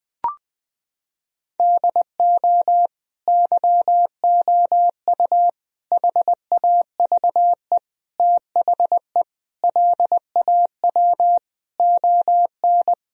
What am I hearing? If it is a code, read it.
Morse code, heard as a keyed tone, about 20 wpm, 700 Hz. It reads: DO YOU HAVE THE LAW ON
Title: Sentences from Top 1000 Words - Review Part 3 - 20wpm